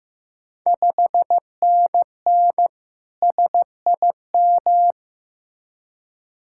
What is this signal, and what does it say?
Morse code, heard as a keyed tone, about 15 wpm, 700 Hz. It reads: 5NN SIM
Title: Ontario QSO Party Exchange 15wpm